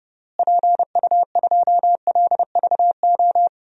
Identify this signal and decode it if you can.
Morse code, heard as a keyed tone, about 30 wpm, 700 Hz. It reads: PU2LVO
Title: DXCC-100 Call Signs - 6 Characters - Repeated 3x - 30wpm